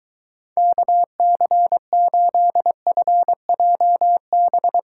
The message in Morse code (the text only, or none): KC8FJB